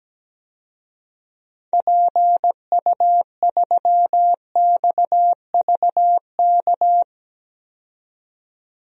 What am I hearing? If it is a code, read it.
Morse code, heard as a keyed tone, about 17 wpm, 700 Hz. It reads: PU3XVK
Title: DXCC-100 Call Signs - 6 Characters - 17wpm